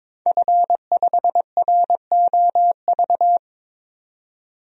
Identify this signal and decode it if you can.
Morse code, heard as a keyed tone, about 22 wpm, 700 Hz. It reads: F5ROV
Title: DXCC-100 Call Signs - 5 Characters - 22wpm